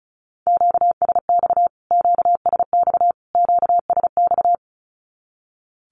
Morse code, heard as a keyed tone, about 34 wpm, 700 Hz. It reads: QSX QSX QSX
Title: Q-Codes Repeated 3x 35wpm